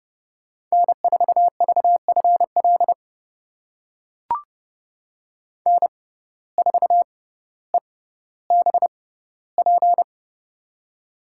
Morse code, heard as a keyed tone, about 30 wpm, spaced as for 15 wpm, 700 Hz. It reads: N4VFL N4EBP
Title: USA Call Signs - 5 Characters - 6x Character Spacing - 30wpm